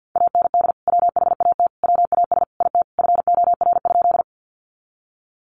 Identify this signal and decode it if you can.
Morse code, heard as a keyed tone, about 50 wpm, 700 Hz. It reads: AND WHAT WAS IT FOR?
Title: Sentences from Top 100 Words 50wpm